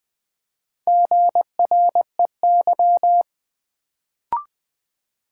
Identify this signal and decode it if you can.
Morse code, heard as a keyed tone, about 20 wpm, 700 Hz. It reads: GREY
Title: Build A Word - 1 Letter At A Time - 3 to 5-Letter Words - 20wpm